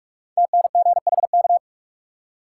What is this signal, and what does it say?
Morse code, heard as a keyed tone, about 45 wpm, 700 Hz. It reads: TNGHK
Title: Code Groups - 5 Letter:Numbers - 45wpm